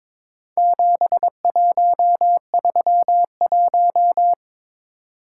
Morse code, heard as a keyed tone, about 22 wpm, 700 Hz. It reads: 7131
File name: practice / 22wpm / Four Letter-Number Combinations - 22wpm